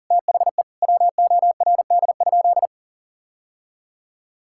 Morse code, heard as a keyed tone, about 40 wpm, 700 Hz. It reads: THE WORD?
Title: Sentences from Top 1000 Words - Review Part 6 - 40wpm